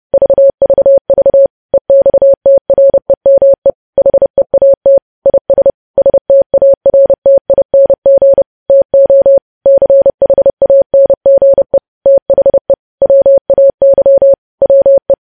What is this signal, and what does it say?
Morse code, heard as a keyed tone, about 30 wpm, 550 Hz. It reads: VVV EXTREME HEAT IS STARTING TO CHANGE THE WAY WE
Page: news headlines in morse code at 30 wpm